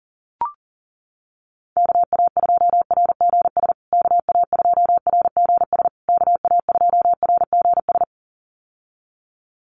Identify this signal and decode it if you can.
Morse code, heard as a keyed tone, about 40 wpm, 700 Hz. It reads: KA2RGS KA2RGS KA2RGS
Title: USA Call Signs Repeated 3x 40wpm